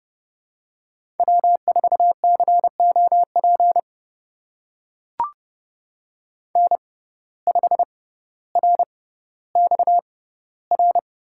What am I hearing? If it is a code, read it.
Morse code, heard as a keyed tone, about 30 wpm, spaced as for 15 wpm, 700 Hz. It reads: W4COP N5RXR
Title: USA Call Signs - 5 Characters - 6x Character Spacing - 30wpm